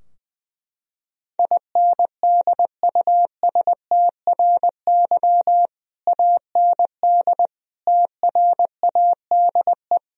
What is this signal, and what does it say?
Morse code, heard as a keyed tone, about 21 wpm, 700 Hz. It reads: INDUSTRY AND TRADE
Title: Sentences from Top 1000 Words - Review Part 4 - 20wpm